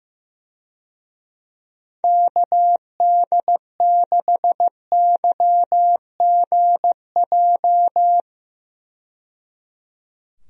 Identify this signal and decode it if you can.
Morse code, heard as a keyed tone, about 15 wpm, 700 Hz. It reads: KD6YGJ
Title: USA Call Signs - 6 Characters - 15wpm